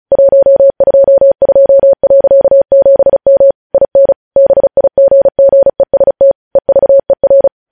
Morse code, heard as a keyed tone, about 36 wpm, 550 Hz. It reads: 122.7M IN BIGGEST EVER
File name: podcast / News headlines-1x-35x35